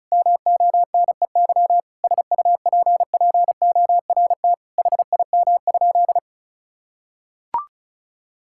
Morse code, heard as a keyed tone, about 36 wpm, 700 Hz. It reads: MONEY SUPPORT HIM?
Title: Sentences from Top 1000 Words - Review Part 5 - 35wpm